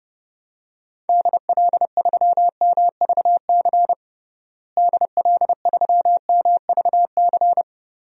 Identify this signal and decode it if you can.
Morse code, heard as a keyed tone, about 29 wpm, 700 Hz. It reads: DL3MVC DL3MVC
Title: DXCC-100 Call Signs - 6 Characters - Repeated 3x - 30wpm